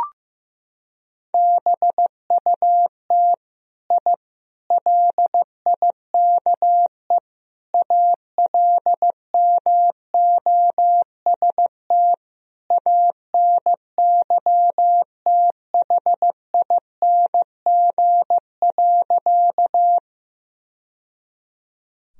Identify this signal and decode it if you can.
Morse code, heard as a keyed tone, about 15 wpm, 700 Hz. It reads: BUT I LIKE ALMOST ANYTHING.
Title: Sentences from Top 1000 Words - Review Part 4 - 15wpm